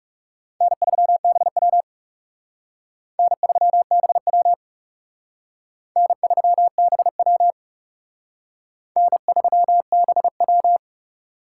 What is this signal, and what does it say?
Morse code, heard as a keyed tone, about 38 wpm, 700 Hz. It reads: N3BW N3BW N3BW N3BW